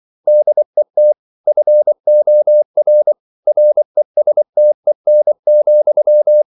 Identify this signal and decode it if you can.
Morse code, heard as a keyed tone, about 24 wpm, 600 Hz. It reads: DET FOR RESTEN,
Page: Ferie 24 wpm